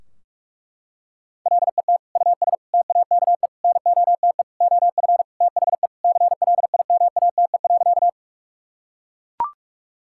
Morse code, heard as a keyed tone, about 41 wpm, 700 Hz. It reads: LET US TAKE NOTE OF THE CLIMATE.
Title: Sentences from Top 1600 Words 45wpm